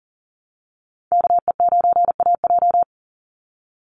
Morse code, heard as a keyed tone, about 40 wpm, 700 Hz. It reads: KE9AJ
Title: NA SOTA Chaser Callsigns 40wpm